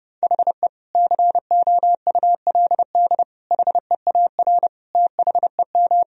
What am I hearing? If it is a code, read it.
Morse code, heard as a keyed tone, about 30 wpm, 700 Hz. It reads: HE COULD HEAR THEM
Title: Sentences from Top 1000 Words - Review Part 1 - 30wpm